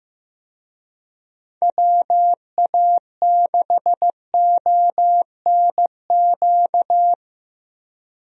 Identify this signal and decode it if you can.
Morse code, heard as a keyed tone, about 15 wpm, 700 Hz. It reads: WA6ONQ
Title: USA Call Signs - 6 Characters - Repeated 2x - 15wpm